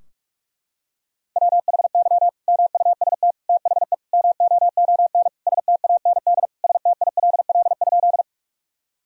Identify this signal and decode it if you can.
Morse code, heard as a keyed tone, about 46 wpm, 700 Hz. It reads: WHY MUST THE MOON STAND STILL?
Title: Sentences from Top 1000 Words - Review Part 3 - 45wpm